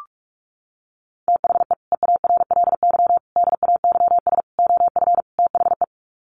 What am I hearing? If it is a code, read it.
Morse code, heard as a keyed tone, about 44 wpm, 700 Hz. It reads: THE EARLY DAYS OF THE
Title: Sets of 5 Words Encore 45wpm